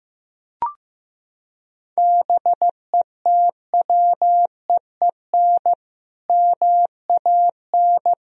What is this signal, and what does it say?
Morse code, heard as a keyed tone, about 15 wpm, 700 Hz. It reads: BETWEEN MAN
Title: Sentences from Top 1000 Words - Review Part 1 - 15wpm